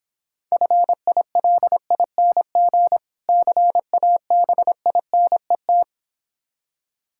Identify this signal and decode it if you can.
Morse code, heard as a keyed tone, about 26 wpm, 700 Hz. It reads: FILING CABINET